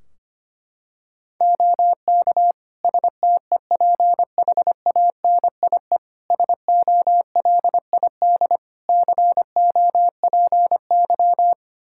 Morse code, heard as a keyed tone, about 25 wpm, 700 Hz. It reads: OK STEPHANIE SOLID COPY
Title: Practice QSO - Exchange 3 - More Info - 25wpm